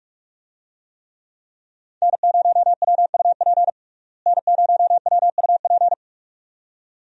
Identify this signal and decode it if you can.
Morse code, heard as a keyed tone, about 46 wpm, 700 Hz. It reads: N0WUP N0WUP
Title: USA Call Signs - 5 Characters - Repeated 2x - 45wpm